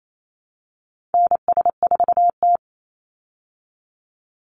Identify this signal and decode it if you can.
Morse code, heard as a keyed tone, about 28 wpm, 690 Hz. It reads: NS4T
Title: Call Signs Repeated 3x - 28wpm